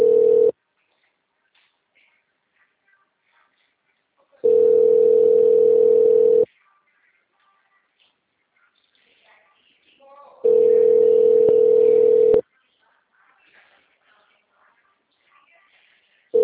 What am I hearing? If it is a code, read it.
Ringback tone (North America), 440 + 480 Hz.